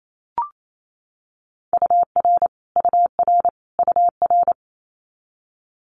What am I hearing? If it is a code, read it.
Morse code, heard as a keyed tone, about 28 wpm, 700 Hz. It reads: UR UR UR
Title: Abbreviations Repeated 3x - 28wpm